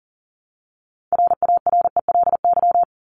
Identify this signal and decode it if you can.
Morse code, heard as a keyed tone, about 40 wpm, 700 Hz. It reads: RARELY
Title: LY Suffix - Top 100 Words with Base Words 40wpm